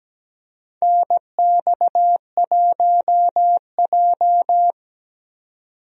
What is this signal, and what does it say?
Morse code, heard as a keyed tone, about 17 wpm, 700 Hz. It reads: NX1J
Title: USA Call Signs - 4 Characters - 17wpm